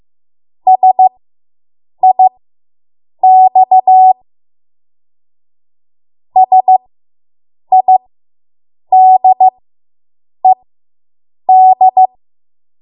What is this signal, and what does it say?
Morse code, heard as a keyed tone, about 15 wpm, spaced as for 7 wpm, 750 Hz. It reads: SIX SIDED